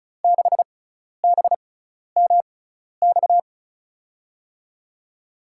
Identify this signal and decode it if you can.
Morse code, heard as a keyed tone, about 36 wpm, spaced as for 12 wpm, 700 Hz. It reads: 6BMX
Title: USA Call Signs - 6x Character Spacing - 35wpm